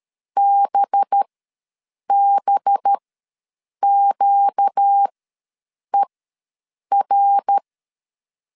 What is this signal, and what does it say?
Morse code, heard as a keyed tone, about 13 wpm, spaced as for 7 wpm, 800 Hz. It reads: BBQER
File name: LEZIONI / 5 LBVC / 5 LB ws 12wpm 3034